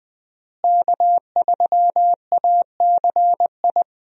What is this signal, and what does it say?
Morse code, heard as a keyed tone, about 20 wpm, 700 Hz. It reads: K3ACI
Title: USA Call Signs - 5 Characters - Repeated 2x - 20wpm